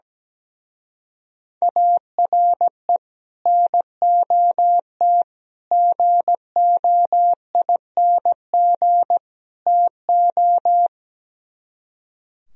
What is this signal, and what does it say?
Morse code, heard as a keyed tone, about 17 wpm, 700 Hz. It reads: ARE NOT GOING TO
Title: Speed-Racing - Sets of 4 Words 17wpm